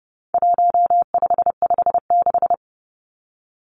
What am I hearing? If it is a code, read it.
Morse code, heard as a keyed tone, about 30 wpm, 700 Hz. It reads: 1556
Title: Numbers - 4-Digit 30wpm